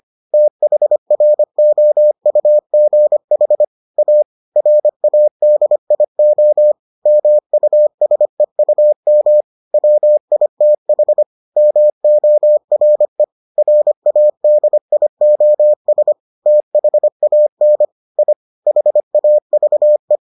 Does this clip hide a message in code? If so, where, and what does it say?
Yes: Morse code from 0.3 to 20.2 s, heard as a keyed tone, about 25 wpm, 600 Hz. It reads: THROUGH A RADIO MUSEUM WITH MORE RADIOS THAN I HAVE